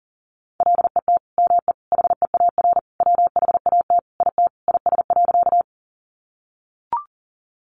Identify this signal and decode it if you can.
Morse code, heard as a keyed tone, about 40 wpm, 700 Hz. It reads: LET ME HEAR WHAT IT IS.
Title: Sentences from Top 300 Words 40wpm